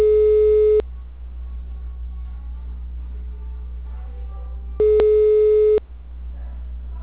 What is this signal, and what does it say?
Ringback tone (Europe, 425 Hz).
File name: out- -2020- - - .28348